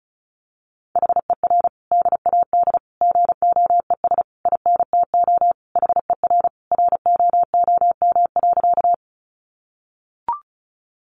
Morse code, heard as a keyed tone, about 34 wpm, 700 Hz. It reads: HER DAD GOES INTO HER ROOM.
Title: Additional Easy Phrases 35wpm